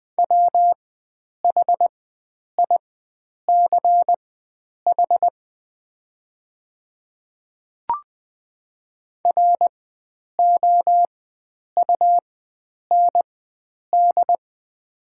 Morse code, heard as a keyed tone, about 20 wpm, spaced as for 9 wpm, 700 Hz. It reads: WHICH ROUND
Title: 4x Character Spacing - Top 300 Words - 20wpm